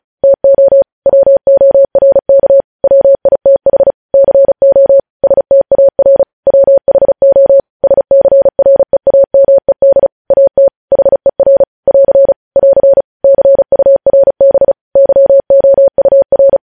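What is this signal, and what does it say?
Morse code, heard as a keyed tone, about 36 wpm, 550 Hz. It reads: TO WORK WITH CO STAR WHO SCREAMED AT HER <AR> <AR> CURB YOUR
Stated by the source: News headlines sent in Morse code at 35 WPM.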